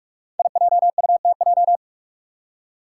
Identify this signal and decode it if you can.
Morse code, heard as a keyed tone, about 44 wpm, 700 Hz. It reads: IJUTJ
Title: Five Challenging Letter-Number Combinations 45wpm